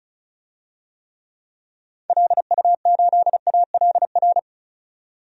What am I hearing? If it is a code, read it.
Morse code, heard as a keyed tone, about 36 wpm, 700 Hz. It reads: LU8ALR
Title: DXCC-100 Call Signs - 6 Characters - 35wpm